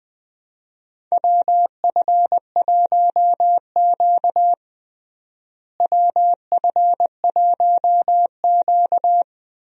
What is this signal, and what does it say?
Morse code, heard as a keyed tone, about 21 wpm, 700 Hz. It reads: WF1Q WF1Q
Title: USA Call Signs - 4 Characters - Repeated 3x - 20wpm